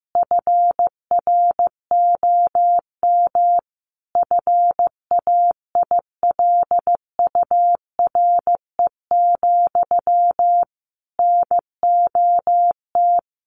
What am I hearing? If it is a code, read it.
Morse code, heard as a keyed tone, about 15 wpm, 700 Hz. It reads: FROM FAILURE, NOT
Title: Sentences from Top 900 Words 15wpm